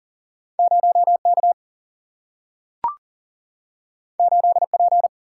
Morse code, heard as a keyed tone, about 40 wpm, 700 Hz. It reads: 0 K 8 P
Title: Single-Letter Single-Number 40wpm